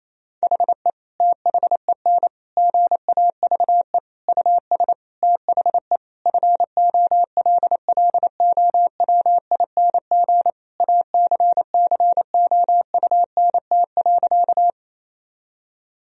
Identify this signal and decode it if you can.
Morse code, heard as a keyed tone, about 28 wpm, 700 Hz. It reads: HE THEN GAVE US THE FOLLOWING ACCOUNT.
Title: Sentences from Top 1000 Words - Review Part 4 - 28wpm